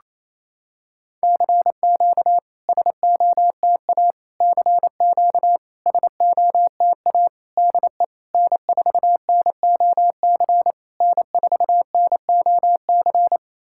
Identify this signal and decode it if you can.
Morse code, heard as a keyed tone, about 28 wpm, 700 Hz. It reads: CQ SOTA CQ SOTA DE N4NOC N4NOC
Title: SOTA QSO Practice 28wpm